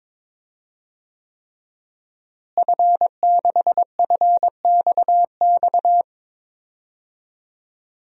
Morse code, heard as a keyed tone, about 22 wpm, 700 Hz. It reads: F6FXX
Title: DXCC-100 Call Signs - 5 Characters - Repeated 3x - 22wpm